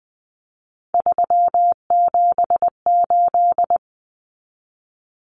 Morse code, heard as a keyed tone, about 20 wpm, 700 Hz. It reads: 378
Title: Common Cut Numbers - 3-Digit 20wpm